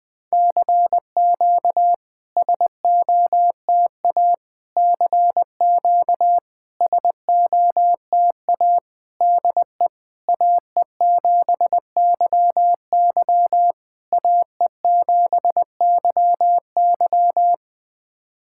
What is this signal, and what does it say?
Morse code, heard as a keyed tone, about 20 wpm, 700 Hz. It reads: CQ SOTA CQ SOTA DE AE7YY AE7YY
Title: SOTA QSO Practice 20wpm